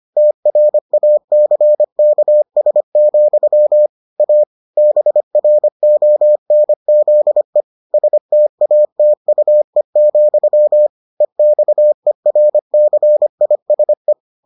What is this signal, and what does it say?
Morse code, heard as a keyed tone, about 25 wpm, 600 Hz. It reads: TRACKS, A BRONZE STATUE, EXERCISE